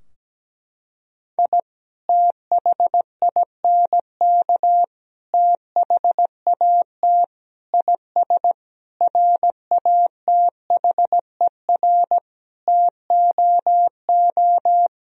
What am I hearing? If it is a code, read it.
Morse code, heard as a keyed tone, about 17 wpm, 700 Hz. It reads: I THINK THAT IS RATHER TOO
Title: Sentences from Top 1000 Words - Review Part 4 - 17wpm